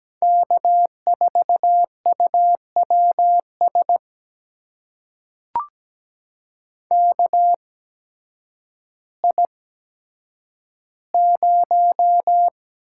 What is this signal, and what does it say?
Morse code, heard as a keyed tone, about 17 wpm, spaced as for 9 wpm, 700 Hz. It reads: K4UWS KI0
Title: USA Call Signs - 8x Character Spacing - 17wpm